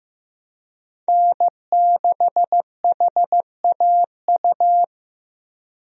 Morse code, heard as a keyed tone, about 16 wpm, 700 Hz. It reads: N6HAU
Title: USA Call Signs - 5 Characters - Repeated 2x - 15wpm